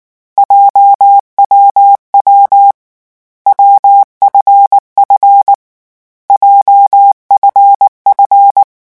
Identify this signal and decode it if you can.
Morse code, heard as a keyed tone, about 19 wpm, 800 Hz. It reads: JWW WFF JFF
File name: morse mp3 / lecon7-vitesse 18